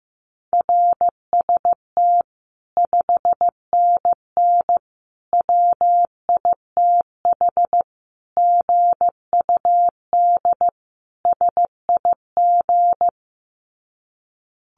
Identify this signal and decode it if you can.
Morse code, heard as a keyed tone, about 14 wpm, 700 Hz. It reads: RST 5NN WITH GUD SIG
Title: Common QSO Phrases 15wpm